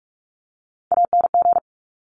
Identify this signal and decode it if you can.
Morse code, heard as a keyed tone, about 46 wpm, 700 Hz. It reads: ANG
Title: Top Three Letter Combinations in Words 45wpm